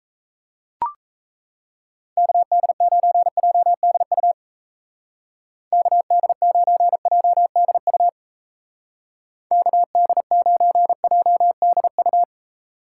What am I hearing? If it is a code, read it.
Morse code, heard as a keyed tone, about 36 wpm, 700 Hz. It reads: KD9JDU KD9JDU KD9JDU